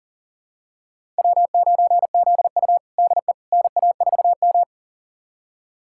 Morse code, heard as a keyed tone, about 40 wpm, 700 Hz. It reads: W9ZU DE NA4M
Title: USA Callsign DE Callsign - 4 Characters - 40wpm